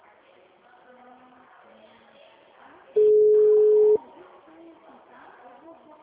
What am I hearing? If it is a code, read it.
Ringback tone (Europe, 425 Hz), with speech.